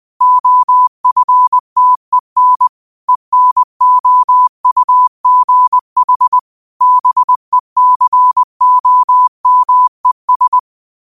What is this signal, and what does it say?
Morse code, heard as a keyed tone, about 20 wpm, 1000 Hz. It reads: OFTEN ENOUGH BECOMES
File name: qotdfiles / 20wpm